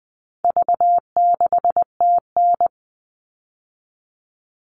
Morse code, heard as a keyed tone, about 20 wpm, 700 Hz. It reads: V6TN
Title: All Cut Numbers - 4-Digit 20wpm